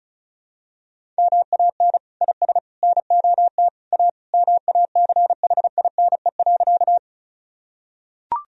Morse code, heard as a keyed tone, about 36 wpm, 700 Hz. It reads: MAN IS NOT A MACHINE.
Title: Sentences from Top 1000 Words - Review Part 2 - 35wpm